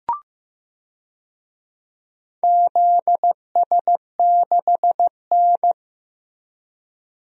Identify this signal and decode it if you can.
Morse code, heard as a keyed tone, about 15 wpm, 700 Hz. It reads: ZS6N
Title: DXCC-100 Call Signs - 4 Characters - Repeated 3x - 15wpm